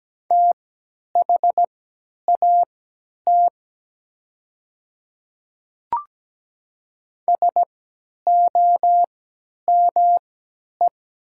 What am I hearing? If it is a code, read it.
Morse code, heard as a keyed tone, about 17 wpm, spaced as for 10 wpm, 700 Hz. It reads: THAT SOME
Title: 3x Character Spacing - Top 200 Words - 17wpm